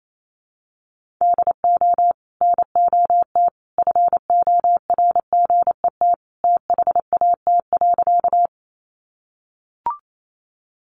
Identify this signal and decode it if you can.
Morse code, heard as a keyed tone, about 28 wpm, 700 Hz. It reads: DO NOT FORGET THAT.
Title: Sentences from Top 1200 Words 28wpm